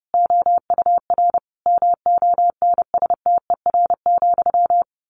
Morse code, heard as a keyed tone, about 30 wpm, 700 Hz. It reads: OUR MONSTER,
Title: A Sound Of Thunder 30wpm